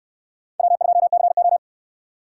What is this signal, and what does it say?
Morse code, heard as a keyed tone, about 66 wpm, 700 Hz. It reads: W2CO